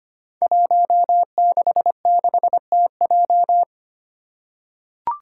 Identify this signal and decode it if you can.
Morse code, heard as a keyed tone, about 25 wpm, 700 Hz. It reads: 166TJ
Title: Five Challenging Letter-Number Combinations 25wpm